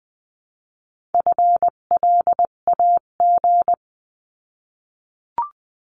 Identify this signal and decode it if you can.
Morse code, heard as a keyed tone, about 20 wpm, 700 Hz. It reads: FLAG
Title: 258 Lesson 42c - Words 16-20wpm